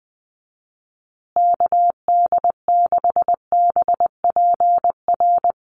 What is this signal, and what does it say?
Morse code, heard as a keyed tone, about 20 wpm, 700 Hz. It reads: KD6BPR
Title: USA Call Signs Repeated 2x 20wpm